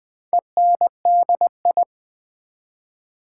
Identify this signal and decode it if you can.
Morse code, heard as a keyed tone, about 20 wpm, 700 Hz. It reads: ENDI
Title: Build A Word - 2 Letters At A Time - 6-Letter Words - 20wpm